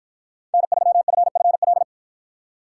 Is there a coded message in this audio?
Yes: Morse code from 0.5 to 1.8 s, heard as a keyed tone, about 53 wpm, 700 Hz. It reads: N3FFL